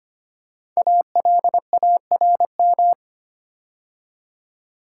Morse code, heard as a keyed tone, about 25 wpm, 700 Hz. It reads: ALARM
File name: 5 Letter Words - First 3 Letters Given - 25wpm